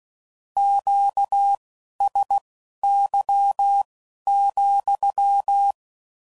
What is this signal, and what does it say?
Morse code, heard as a keyed tone, about 16 wpm, spaced as for 11 wpm, 800 Hz. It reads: QSY,
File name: CW Code Groups @15WPM / Q Calls / Q Calls sppc